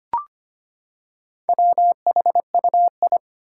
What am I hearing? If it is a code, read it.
Morse code, heard as a keyed tone, about 25 wpm, 700 Hz. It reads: WHUI
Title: Four Challenging Letter-Number Combinations 25wpm